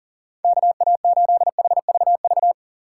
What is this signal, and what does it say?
Morse code, heard as a keyed tone, about 40 wpm, 700 Hz. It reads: KA8HVV
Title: USA Call Signs - 6 Characters - 6x Character Spacing - 40wpm